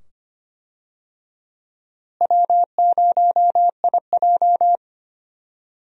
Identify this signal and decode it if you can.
Morse code, heard as a keyed tone, about 25 wpm, 700 Hz. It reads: W0IJ
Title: USA Call Signs - 4 Characters - Repeated 2x - 25wpm